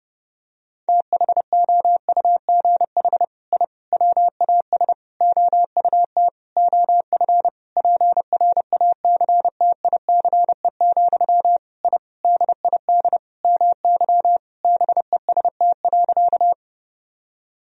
Morse code, heard as a keyed tone, about 30 wpm, 700 Hz. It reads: THOUGH I WAS OUT OF PRACTICE, I DID MY BEST.
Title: Sentences from Top 1000 Words - Review Part 4 - 30wpm